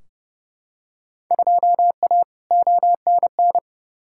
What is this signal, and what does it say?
Morse code, heard as a keyed tone, about 30 wpm, 700 Hz. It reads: 2A ONN
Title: Field Day Exchanges Encore - 30wpm